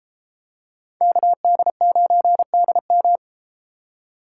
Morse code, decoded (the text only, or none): KD9DM